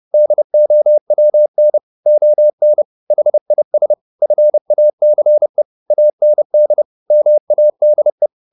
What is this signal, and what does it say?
Morse code, heard as a keyed tone, about 30 wpm, 600 Hz. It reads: DOWN ON HIS FACE AND MADE
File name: the old man and the sea / the old man and the sea 1 30wpm